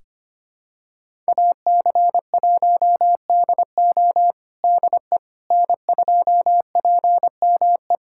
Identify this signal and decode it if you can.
Morse code, heard as a keyed tone, about 25 wpm, 700 Hz. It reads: AC1DO DE N2PME
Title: USA Callsign DE Callsign - 5 Characters - 25wpm